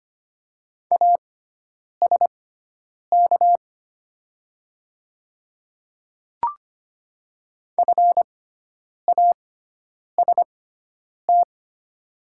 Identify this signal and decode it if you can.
Morse code, heard as a keyed tone, about 25 wpm, spaced as for 9 wpm, 700 Hz. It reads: ASK FAST
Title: 6x Character Spacing - Top 500 Words - 25wpm